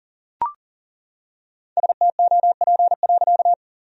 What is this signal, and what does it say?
Morse code, heard as a keyed tone, about 40 wpm, 700 Hz. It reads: STOP.
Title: Sentences from Top 1000 Words - Review Part 1 - 40wpm